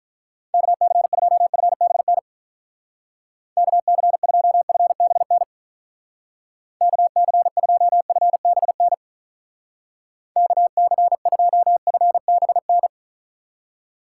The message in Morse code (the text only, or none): KC2FBN KC2FBN KC2FBN KC2FBN